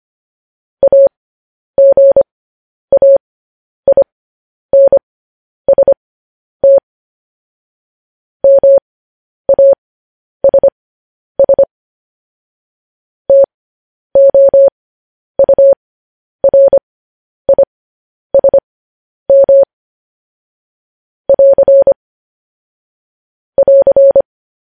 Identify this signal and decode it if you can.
Morse code, heard as a keyed tone, about 25 wpm, spaced as for 10 wpm, 550 Hz. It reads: AGAINST MASS TOURISM <AR> <AR>